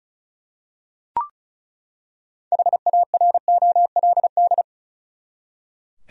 Morse code, heard as a keyed tone, about 36 wpm, 700 Hz. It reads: HAROLD
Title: USA Common Names - 35wpm